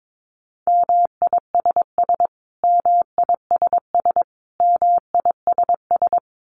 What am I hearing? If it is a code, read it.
Morse code, heard as a keyed tone, about 22 wpm, 700 Hz. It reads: MISS MISS MISS
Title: Top 500 Words Repeated 22wpm